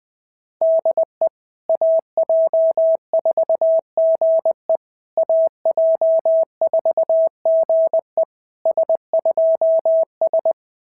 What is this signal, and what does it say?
Morse code, heard as a keyed tone, about 21 wpm, 690 Hz. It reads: DE AJ4GE AJ4GE S2S
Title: SOTA QSO Practice 20wpm